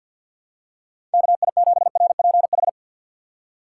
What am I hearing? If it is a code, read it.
Morse code, heard as a keyed tone, about 52 wpm, 700 Hz. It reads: KI7RPH
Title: USA Call Signs - 6 Characters - 3x Character Spacing - 50wpm